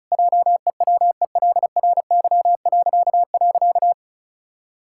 Morse code, heard as a keyed tone, about 34 wpm, 700 Hz. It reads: JEWELRY..
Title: Pangram Challenge v2 - 35wpm